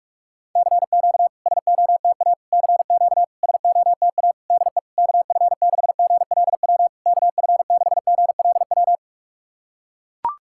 Morse code, heard as a keyed tone, about 42 wpm, 690 Hz. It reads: CQ SOTA CQ SOTA DE KF6GLW KF6GLW
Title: SOTA QSO Practice 45wpm